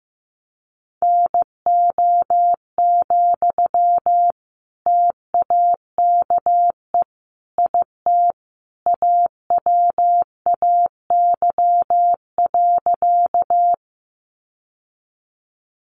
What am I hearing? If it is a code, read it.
Morse code, heard as a keyed tone, about 15 wpm, 700 Hz. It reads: NO, TAKE IT AWAY.
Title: Sentences from Top 600 Words 15wpm